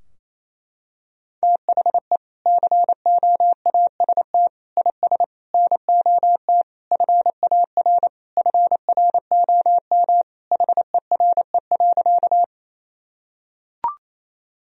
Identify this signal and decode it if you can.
Morse code, heard as a keyed tone, about 28 wpm, 700 Hz. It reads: THE COAST IS NOT FAR FROM HERE.
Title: Sentences from Top 1000 Words - Review Part 4 - 28wpm